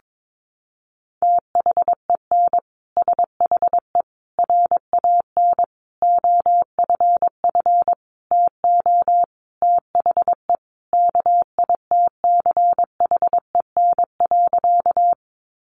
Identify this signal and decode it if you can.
Morse code, heard as a keyed tone, about 22 wpm, 700 Hz. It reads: THEN SHE RAN OFF TO THE KITCHEN.
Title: Sentences from Top 1400 Words 22wpm